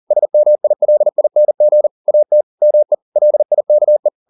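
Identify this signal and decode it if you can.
Morse code, heard as a keyed tone, about 40 wpm, 600 Hz. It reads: SMILING AT ME LIKE
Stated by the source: Morse code at 40 wpm, practice text, CW med Gnister.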